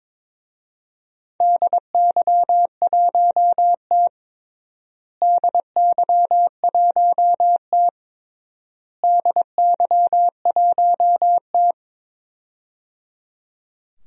Morse code, heard as a keyed tone, about 21 wpm, 700 Hz. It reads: DY1T DY1T DY1T
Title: DXCC-100 Call Signs - 4 Characters - Repeated 3x - 22wpm